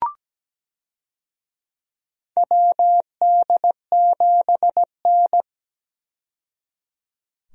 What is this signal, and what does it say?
Morse code, heard as a keyed tone, about 17 wpm, 700 Hz. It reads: WD7N
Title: USA Call Signs - 4 Characters - 17wpm